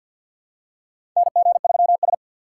Morse code, heard as a keyed tone, about 46 wpm, 700 Hz. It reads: NG3S
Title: USA Call Signs - 2x Character Spacing - 50wpm